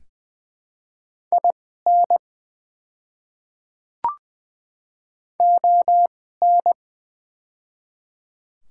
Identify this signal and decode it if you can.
Morse code, heard as a keyed tone, about 20 wpm, spaced as for 14 wpm, 700 Hz. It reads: IN ON
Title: 2x Character Spacing - Top 100 Words - 20wpm